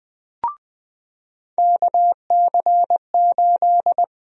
Morse code, heard as a keyed tone, about 20 wpm, 700 Hz. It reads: KC8
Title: USA Call Sign Prefixes - 20wpm